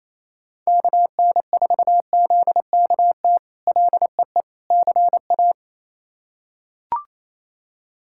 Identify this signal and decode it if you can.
Morse code, heard as a keyed tone, about 28 wpm, 700 Hz. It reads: KN4ZKT LEE CA
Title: K1USN SST QSO Practice - 28wpm